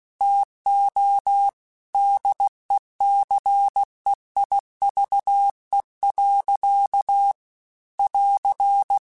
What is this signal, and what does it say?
Morse code, heard as a keyed tone, about 16 wpm, 800 Hz. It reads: TO DECEIVE. <AR>
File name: Quotes / Quotes-mp3s / Deceive @15